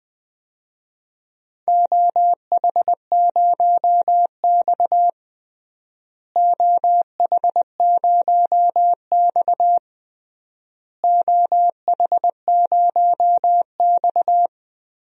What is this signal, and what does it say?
Morse code, heard as a keyed tone, about 21 wpm, 700 Hz. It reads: OH0X OH0X OH0X
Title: DXCC-100 Call Signs - 4 Characters - Repeated 3x - 20wpm